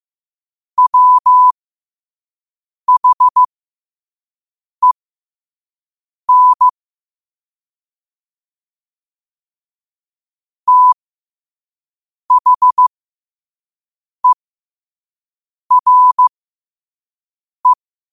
Morse code, heard as a keyed tone, about 15 wpm, spaced as for 5 wpm, 1000 Hz. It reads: WHEN THERE
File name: qotdfiles / 5wpm